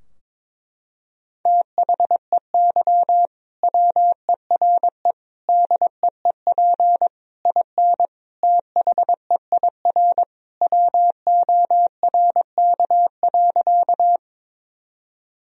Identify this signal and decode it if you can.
Morse code, heard as a keyed tone, about 22 wpm, 700 Hz. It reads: THEY WERE DEEP IN THEIR WORK.
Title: Sentences from Top 1000 Words - Review Part 3 - 22wpm